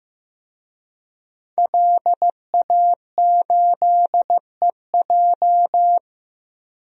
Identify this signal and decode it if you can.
Morse code, heard as a keyed tone, about 15 wpm, 700 Hz. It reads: LA8EJ
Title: DXCC-100 Call Signs - 5 Characters - Repeated 2x - 15wpm